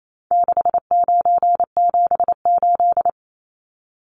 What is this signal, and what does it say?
Morse code, heard as a keyed tone, about 28 wpm, 700 Hz. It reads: 6978
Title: Numbers - 4-Digit 28wpm